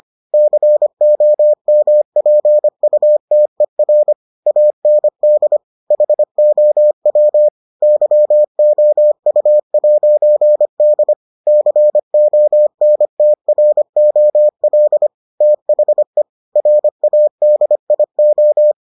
Morse code, heard as a keyed tone, about 25 wpm, 600 Hz. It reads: COMPUTER AND HOW YOU'D CONTROL THE RADIO